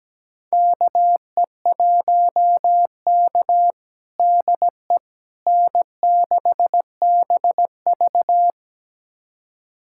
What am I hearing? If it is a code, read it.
Morse code, heard as a keyed tone, about 17 wpm, 700 Hz. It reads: KE1K DE N6BV
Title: USA Callsign DE Callsign - 4 Characters - 17wpm